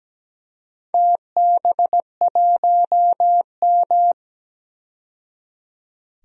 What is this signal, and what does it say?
Morse code, heard as a keyed tone, about 17 wpm, 700 Hz. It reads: TB1M
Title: DXCC-100 Call Signs - 4 Characters - 17wpm